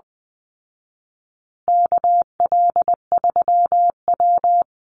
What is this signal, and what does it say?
Morse code, heard as a keyed tone, about 20 wpm, 700 Hz. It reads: KL3W
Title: Call Signs Repeated 2x - 20wpm